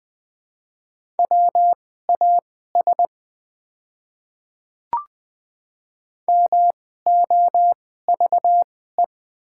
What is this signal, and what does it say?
Morse code, heard as a keyed tone, about 20 wpm, spaced as for 14 wpm, 700 Hz. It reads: WAS MOVE
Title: 2x Character Spacing - Top 300 Words - 20wpm